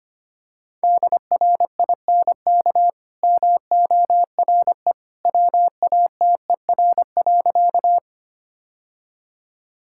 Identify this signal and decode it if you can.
Morse code, heard as a keyed tone, about 25 wpm, 700 Hz. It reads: DRINK MORE WATER.
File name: Easy Sentences with Hints - 25wpm